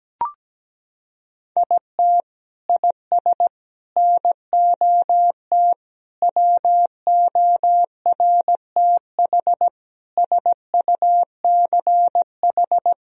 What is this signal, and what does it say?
Morse code, heard as a keyed tone, about 17 wpm, 700 Hz. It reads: IT IS NOT WORTH SUCH
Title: Sentences from Top 1000 Words - Review Part 3 - 17wpm